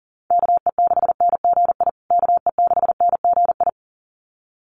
Morse code, heard as a keyed tone, about 40 wpm, 700 Hz. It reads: KE6NGI KE6NGI
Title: USA Call Signs Repeated 2x 40wpm